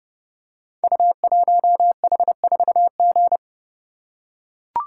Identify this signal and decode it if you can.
Morse code, heard as a keyed tone, about 30 wpm, 700 Hz. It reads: U1H4G
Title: Code Groups - 5 Letter:Numbers - 30wpm